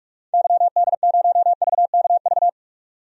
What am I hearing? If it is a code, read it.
Morse code, heard as a keyed tone, about 47 wpm, 700 Hz. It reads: YD0VKV
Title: DXCC-100 Call Signs - 6 Characters - 45wpm